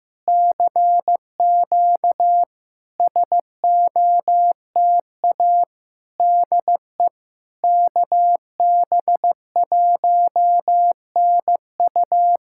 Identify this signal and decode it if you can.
Morse code, heard as a keyed tone, about 15 wpm, 700 Hz. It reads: CQ SOTA DE KB1NU
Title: SOTA QSO Practice 15wpm